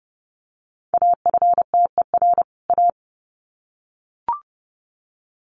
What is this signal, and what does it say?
Morse code, heard as a keyed tone, about 30 wpm, 700 Hz. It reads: AFTER A
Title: Sets of 2 Words 30wpm